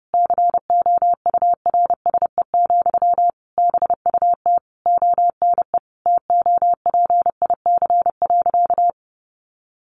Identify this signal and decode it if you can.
Morse code, heard as a keyed tone, about 30 wpm, 700 Hz. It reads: COURSE, BUT ONE TOPIC.
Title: Sentences from Top 1200 Words 30wpm